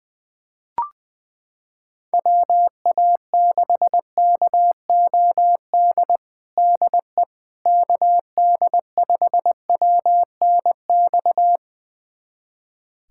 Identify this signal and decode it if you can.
Morse code, heard as a keyed tone, about 21 wpm, 700 Hz. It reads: WA6KOD DE KD5WNX
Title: USA Callsign DE Callsign - 6 Characters - 20wpm